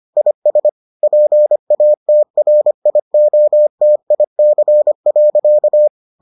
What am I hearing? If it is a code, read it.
Morse code, heard as a keyed tone, about 25 wpm, 600 Hz. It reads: IS PATRIOTIC.